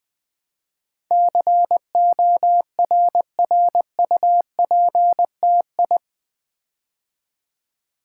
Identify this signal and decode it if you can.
Morse code, heard as a keyed tone, about 20 wpm, 700 Hz. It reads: CORRUPTI
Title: Build A Word - 4 Letters At A Time - 10-Letter Words - 20wpm